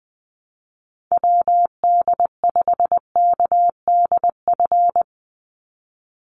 Morse code, heard as a keyed tone, about 20 wpm, 700 Hz. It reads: WD5KDF
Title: USA Call Signs 20wpm